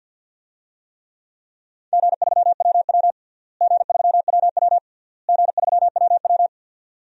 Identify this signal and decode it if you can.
Morse code, heard as a keyed tone, about 52 wpm, 700 Hz. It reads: G3WW G3WW G3WW
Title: DXCC-100 Call Signs - 4 Characters - Repeated 3x - 50wpm